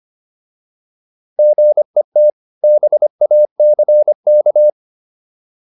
Morse code, heard as a keyed tone, about 25 wpm, 600 Hz. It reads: GET BACK